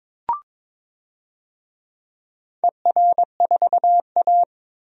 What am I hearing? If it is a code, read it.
Morse code, heard as a keyed tone, about 21 wpm, 700 Hz. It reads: ER4A
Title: DXCC-100 Call Signs - 4 Characters - 22wpm